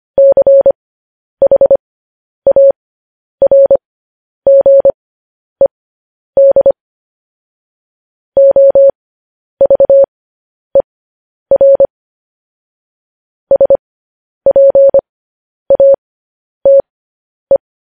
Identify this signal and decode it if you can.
Morse code, heard as a keyed tone, about 25 wpm, spaced as for 10 wpm, 550 Hz. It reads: CHARGED OVER SPATE
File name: podcast / News headlines-1x-25x10